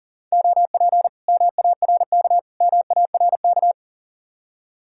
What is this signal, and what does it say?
Morse code, heard as a keyed tone, about 49 wpm, 700 Hz. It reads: OP MARK MARK
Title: Speed-Racing Common QSO Phrases Encore - 40wpm